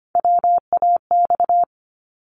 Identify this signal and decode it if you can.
Morse code, heard as a keyed tone, about 25 wpm, 700 Hz. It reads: WAX
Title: Multi-letter Phonemes 25wpm